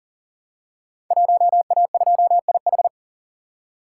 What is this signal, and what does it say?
Morse code, heard as a keyed tone, about 41 wpm, 700 Hz. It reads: 1A2IH
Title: Code Groups - 5 Letter:Numbers - 40wpm